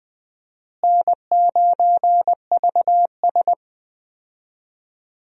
Morse code, heard as a keyed tone, about 20 wpm, 700 Hz. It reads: N9VS
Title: USA Call Signs - 4 Characters - Repeated 2x - 20wpm